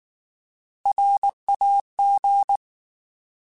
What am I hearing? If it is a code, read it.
Morse code, heard as a keyed tone, about 19 wpm, 800 Hz. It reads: RAG
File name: Alphabetical / Alphabetical mp3s / Rs @18